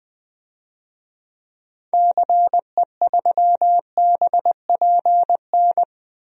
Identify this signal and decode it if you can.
Morse code, heard as a keyed tone, about 20 wpm, 700 Hz. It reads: CE3BPN
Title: DXCC-100 Call Signs - 6 Characters - 20wpm